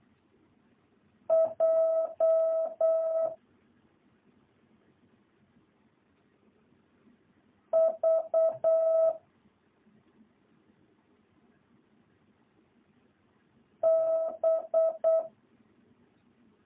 Morse code, heard as a keyed tone, about 8 wpm, 650 Hz. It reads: J V B